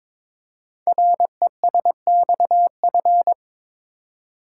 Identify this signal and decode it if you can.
Morse code, heard as a keyed tone, about 22 wpm, 700 Hz. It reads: RESXF
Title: Code Groups - 5 Letters - 22wpm